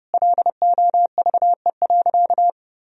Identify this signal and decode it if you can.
Morse code, heard as a keyed tone, about 32 wpm, 700 Hz. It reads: LOVE.
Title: Sentences from Top 1000 Words - Review Part 3 - 30wpm